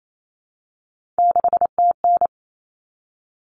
Morse code, heard as a keyed tone, about 28 wpm, 700 Hz. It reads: 6TN
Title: Common Cut Numbers - 3-Digit 28wpm